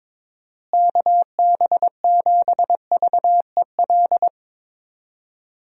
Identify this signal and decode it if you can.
Morse code, heard as a keyed tone, about 22 wpm, 700 Hz. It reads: KB7VEL
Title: USA Call Signs - 6 Characters - Repeated 3x - 22wpm